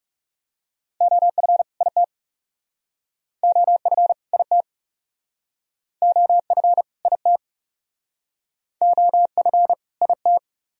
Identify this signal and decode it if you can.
Morse code, heard as a keyed tone, about 36 wpm, 700 Hz. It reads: OF IT OF IT OF IT OF IT
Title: Speed-Racing - Sets of 2 Words 30wpm